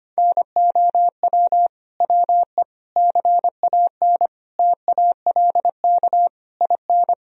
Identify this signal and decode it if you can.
Morse code, heard as a keyed tone, about 25 wpm, 700 Hz. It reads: NOW WE CAN TALK IN
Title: Sentences from Top 1000 Words - Review Part 4 - 25wpm